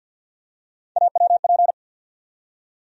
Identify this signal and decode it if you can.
Morse code, heard as a keyed tone, about 53 wpm, 700 Hz. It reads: AWP
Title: USA Call Sign Postfixes - 3 Letters 50wpm